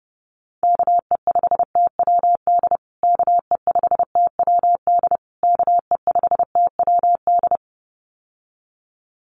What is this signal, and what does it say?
Morse code, heard as a keyed tone, about 30 wpm, 700 Hz. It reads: KE5TWD KE5TWD KE5TWD
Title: USA Call Signs Repeated 3x 30wpm